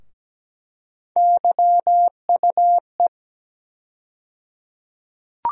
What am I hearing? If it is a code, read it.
Morse code, heard as a keyed tone, about 17 wpm, 700 Hz. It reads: YUE
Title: USA Call Sign Postfixes - 3 Letters 17wpm